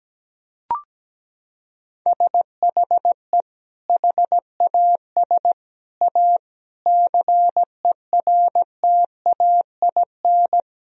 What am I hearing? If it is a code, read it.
Morse code, heard as a keyed tone, about 17 wpm, 700 Hz. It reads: SHE HAS A CERTAIN
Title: Sentences from Top 1000 Words - Review Part 4 - 17wpm